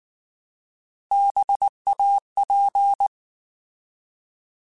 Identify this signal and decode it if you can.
Morse code, heard as a keyed tone, about 19 wpm, 800 Hz. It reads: BAP
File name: Alphabetical / Alphabetical mp3s / Bs @18